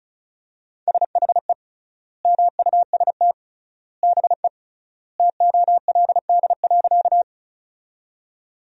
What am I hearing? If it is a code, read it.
Morse code, heard as a keyed tone, about 36 wpm, 700 Hz. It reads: SHE MUST BE TOLD.
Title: 3x Word-Spacing - Sentences from Top 400 Words - 35wpm